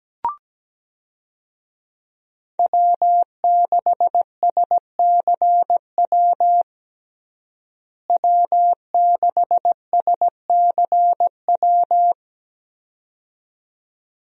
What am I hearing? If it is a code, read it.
Morse code, heard as a keyed tone, about 17 wpm, 700 Hz. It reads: W6SCW W6SCW
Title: USA Call Signs - 5 Characters - Repeated 2x - 17wpm